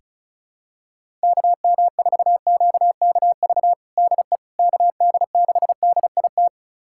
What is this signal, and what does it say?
Morse code, heard as a keyed tone, about 34 wpm, 690 Hz. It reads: KM4QKV DE KD6DIT
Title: USA Callsign DE Callsign - 6 Characters - 35wpm